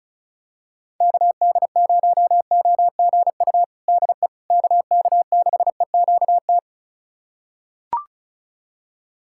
Morse code, heard as a keyed tone, about 36 wpm, 700 Hz. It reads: KD0OGU DE KK6EQT
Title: USA Callsign DE Callsign - 6 Characters - 35wpm